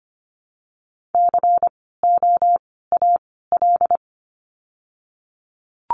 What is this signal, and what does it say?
Morse code, heard as a keyed tone, about 25 wpm, 700 Hz. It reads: C O A L
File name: course / 25:10wpm / 094 Lesson 16f - Letter c - Words 25wpm